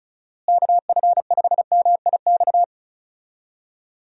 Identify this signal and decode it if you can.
Morse code, heard as a keyed tone, about 34 wpm, 700 Hz. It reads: KF5MIX
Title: USA Call Signs - 6 Characters - Repeated 3x - 35wpm